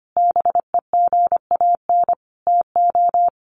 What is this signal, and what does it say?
Morse code, heard as a keyed tone, about 25 wpm, 700 Hz. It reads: BEGAN TO
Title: Sentences from Top 1100 Words 25wpm